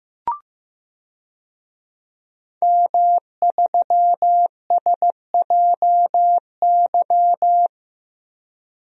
Morse code, heard as a keyed tone, about 15 wpm, 700 Hz. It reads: M3SJY
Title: DXCC-100 Call Signs - 5 Characters - Repeated 3x - 15wpm